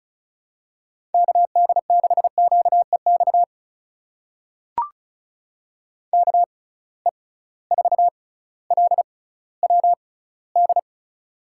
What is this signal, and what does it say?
Morse code, heard as a keyed tone, about 36 wpm, spaced as for 18 wpm, 700 Hz. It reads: KD6QEX KE4LWD
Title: USA Call Signs - 6 Characters - 6x Character Spacing - 35wpm